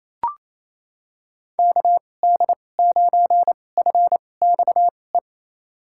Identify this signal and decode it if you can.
Morse code, heard as a keyed tone, about 28 wpm, spaced as for 20 wpm, 700 Hz. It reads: KD9FXE
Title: USA Call Signs - 2x Character Spacing - 28wpm